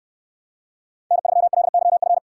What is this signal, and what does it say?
Morse code, heard as a keyed tone, about 69 wpm, 700 Hz. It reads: N3LYF